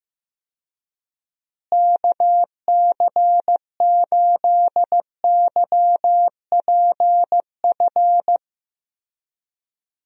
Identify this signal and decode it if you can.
Morse code, heard as a keyed tone, about 15 wpm, 700 Hz. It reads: KC8YPF
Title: USA Call Signs - 6 Characters - Repeated 3x - 15wpm